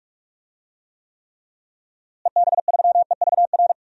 Morse code, heard as a keyed tone, about 41 wpm, 700 Hz. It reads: EB3EVR
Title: DXCC-100 Call Signs - 6 Characters - Repeated 3x - 45wpm